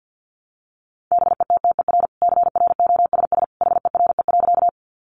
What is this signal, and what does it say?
Morse code, heard as a keyed tone, about 50 wpm, 700 Hz. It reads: BETTER CROSS HERE.
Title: Sentences from Top 400 Words 50wpm